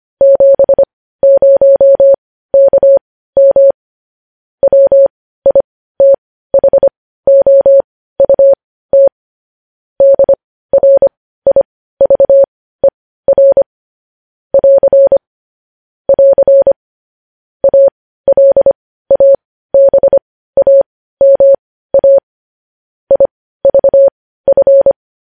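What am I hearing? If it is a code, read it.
Morse code, heard as a keyed tone, about 25 wpm, spaced as for 15 wpm, 550 Hz. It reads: 70KM WITHOUT DRIVER <AR> <AR> ALABAMA IVF
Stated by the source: Morse code news headlines at 15.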